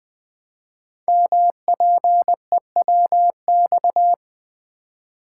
Morse code, heard as a keyed tone, about 20 wpm, 700 Hz. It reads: MPEWX
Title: Code Groups - 5 Letters - 20wpm